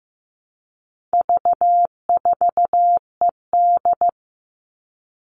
Morse code, heard as a keyed tone, about 15 wpm, 700 Hz. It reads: V4ED
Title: All Cut Numbers - 4-Digit 15wpm